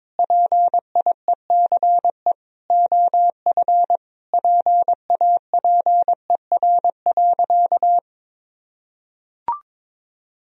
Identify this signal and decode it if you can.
Morse code, heard as a keyed tone, about 22 wpm, 700 Hz. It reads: PIECE OF PAPER.
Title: Sentences from Top 1000 Words - Review Part 2 - 22wpm